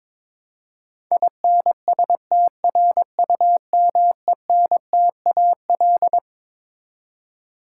Morse code, heard as a keyed tone, about 21 wpm, 700 Hz. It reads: INSTRUMENTAL
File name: Twelve Letter Words - 22wpm